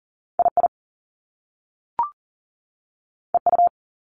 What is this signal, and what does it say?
Morse code, heard as a keyed tone, about 40 wpm, 700 Hz. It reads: II EU
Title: Top Two Letter Combinations in Words 40wpm